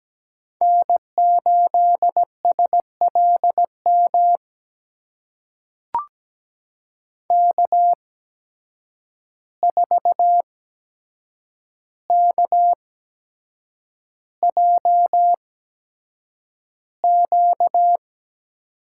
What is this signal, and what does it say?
Morse code, heard as a keyed tone, about 17 wpm, spaced as for 7 wpm, 700 Hz. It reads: N8SLM K4KJQ
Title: USA Call Signs - 5 Characters - 8x Character Spacing - 17wpm